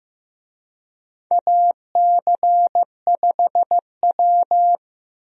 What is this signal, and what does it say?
Morse code, heard as a keyed tone, about 15 wpm, 700 Hz. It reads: AC5W
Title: USA Call Signs - 4 Characters - Repeated 2x - 15wpm